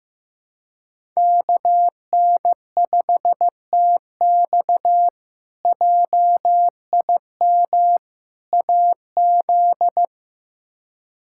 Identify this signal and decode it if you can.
Morse code, heard as a keyed tone, about 15 wpm, 700 Hz. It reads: KN5TX JIM AZ
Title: K1USN SST QSO Practice - 15wpm